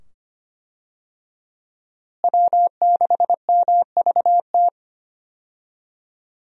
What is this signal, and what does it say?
Morse code, heard as a keyed tone, about 25 wpm, 700 Hz. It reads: W6MVT
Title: USA Call Signs - 5 Characters - Repeated 2x - 25wpm